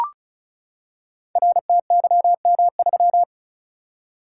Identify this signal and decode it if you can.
Morse code, heard as a keyed tone, about 36 wpm, 700 Hz. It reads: RTYM3
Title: Code Groups - 5 Letter:Numbers - 35wpm